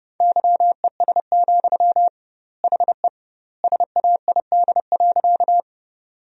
Morse code, heard as a keyed tone, about 30 wpm, 700 Hz. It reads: YES, HE SAID.
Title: 2x Word-Spacing - Sentences from Top 500 Words - 30wpm